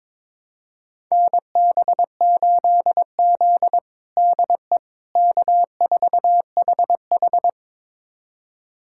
Morse code, heard as a keyed tone, about 22 wpm, 700 Hz. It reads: NB8Z DE K4HH
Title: USA Callsign DE Callsign - 4 Characters - 22wpm